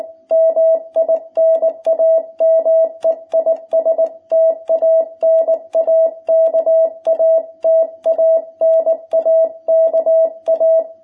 Morse code, heard as a keyed tone, about 19 wpm, 660 Hz. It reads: MI NAME IS TANAKA TANAKA
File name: jh8cbh / 2021cw